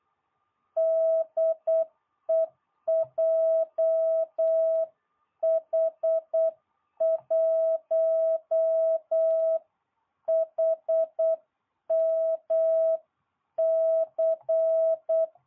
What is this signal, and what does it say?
Morse code, heard as a keyed tone, about 8 wpm, 660 Hz. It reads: DE JH1HMC